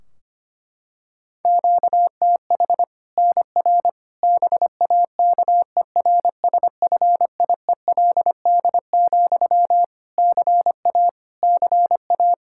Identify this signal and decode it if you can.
Morse code, heard as a keyed tone, about 25 wpm, 700 Hz. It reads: QTH NR BAKERSFIELD, CA CA
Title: Practice QSO - Exchange 1 - Beginning - 25wpm